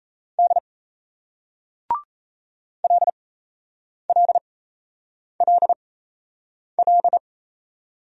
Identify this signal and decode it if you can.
Morse code, heard as a keyed tone, about 38 wpm, spaced as for 8 wpm, 700 Hz. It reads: D LLLL